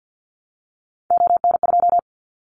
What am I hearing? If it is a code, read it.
Morse code, heard as a keyed tone, about 50 wpm, 700 Hz. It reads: ON2
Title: Callsign Prefixes for Top 50 DXCC - 50wpm